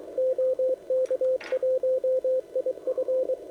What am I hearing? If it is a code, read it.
Morse code, heard as a keyed tone, about 23 wpm, 520 Hz, with radio noise. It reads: OK1IF